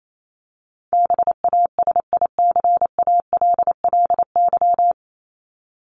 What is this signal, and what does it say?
Morse code, heard as a keyed tone, about 28 wpm, 700 Hz. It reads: BASICALLY
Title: Top 1251-1500 Words 28wpm